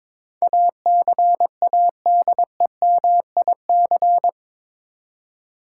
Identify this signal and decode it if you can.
Morse code, heard as a keyed tone, about 22 wpm, 700 Hz. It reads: ACADEMIC
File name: practice / 22wpm / Eight Letter Words - 22wpm